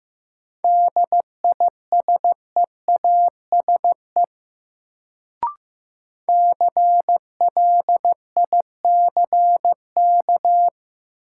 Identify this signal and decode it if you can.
Morse code, heard as a keyed tone, about 15 wpm, 700 Hz. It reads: DISEASE CLICK
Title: Sentences from Top 1100 Words - New Word Drill Alt - 15wpm